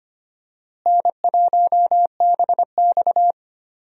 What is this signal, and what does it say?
Morse code, heard as a keyed tone, about 25 wpm, 700 Hz. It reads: N1BX
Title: USA Call Signs - 4 Characters - Repeated 3x - 25wpm